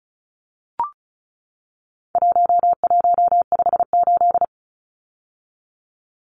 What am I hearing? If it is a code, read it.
Morse code, heard as a keyed tone, about 34 wpm, 700 Hz. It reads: 1158
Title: Numbers - 4-Digit 35wpm